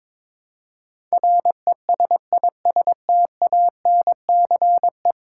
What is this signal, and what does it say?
Morse code, heard as a keyed tone, about 22 wpm, 700 Hz. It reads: RESISTANCE
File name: practice / 22wpm / 10 Letter Words - First 3 Letters Given - 22wpm